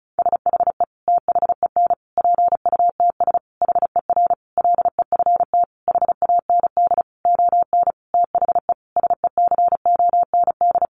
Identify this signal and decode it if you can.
Morse code, heard as a keyed tone, about 34 wpm, 700 Hz. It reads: SHE THEN PUTS HER LEFT HAND ON THE SECOND
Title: Easy Phrases 35wpm